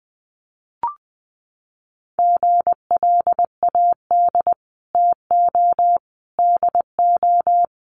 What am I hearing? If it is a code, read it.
Morse code, heard as a keyed tone, about 20 wpm, 700 Hz. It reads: GLAD TO DO
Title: Sentences from Top 800 Words 20wpm